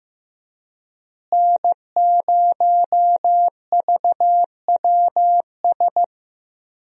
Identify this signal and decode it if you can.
Morse code, heard as a keyed tone, about 15 wpm, 700 Hz. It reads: N0VWS
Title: USA Call Signs - 5 Characters - Repeated 3x - 15wpm